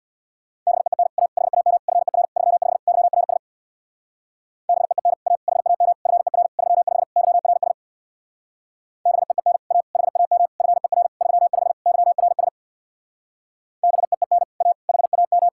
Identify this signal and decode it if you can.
Morse code, heard as a keyed tone, about 62 wpm, 700 Hz. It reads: BEEN A HAM FER 35 YRS BEEN A HAM FER 35 YRS BEEN A HAM FER 35 YRS BEEN A HAM
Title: Speed-Racing Common QSO Phrases - 50wpm